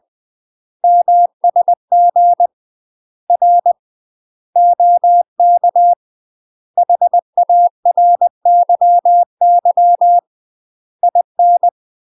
Morse code, heard as a keyed tone, about 20 wpm, 700 Hz. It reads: MSG R OK HARYY IN
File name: UK MASTER QSOs TRAINING / 20WPM / UK MASTER QSOs 0062